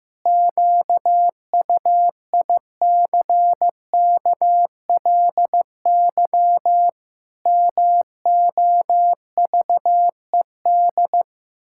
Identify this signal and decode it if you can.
Morse code, heard as a keyed tone, about 15 wpm, 700 Hz. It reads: QUICKLY MOVED
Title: Pangram Challenge v2 - 15wpm